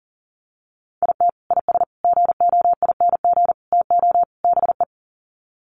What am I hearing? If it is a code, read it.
Morse code, heard as a keyed tone, about 40 wpm, 700 Hz. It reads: IT IS GOING TO BE
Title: Sets of 5 Words 40wpm